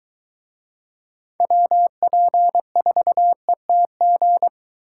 Morse code, heard as a keyed tone, about 22 wpm, 700 Hz. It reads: WP4ETG